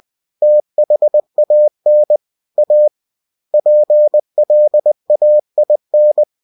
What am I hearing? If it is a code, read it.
Morse code, heard as a keyed tone, about 20 wpm, 600 Hz. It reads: THAN A PLAIN
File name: cw-books / winnie-the-pooh / winnie-20wpm-page-0116